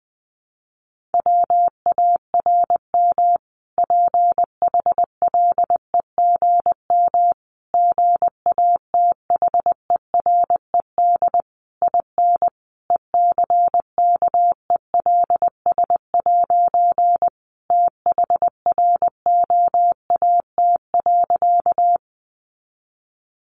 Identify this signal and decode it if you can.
Morse code, heard as a keyed tone, about 20 wpm, 700 Hz. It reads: WARM PHLEGM GATHERED IN ECKELS' THROAT.
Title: A Sound Of Thunder 20wpm